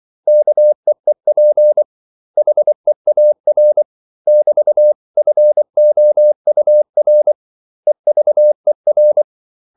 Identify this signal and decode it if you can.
Morse code, heard as a keyed tone, about 24 wpm, 600 Hz. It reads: KEEP HEAR <BT>FOUR EVER
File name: CW / Practice / 500MostCommonEnglishWords24